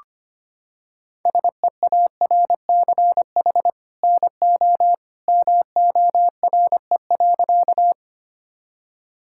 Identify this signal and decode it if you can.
Morse code, heard as a keyed tone, about 25 wpm, 700 Hz. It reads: SEARCH NO MORE.
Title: Sentences from Top 1000 Words - Review Part 5 - 25wpm